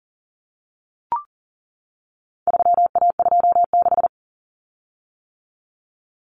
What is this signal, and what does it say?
Morse code, heard as a keyed tone, about 40 wpm, 700 Hz. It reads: 3A26
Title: Common Cut Numbers - 4-Digit 40wpm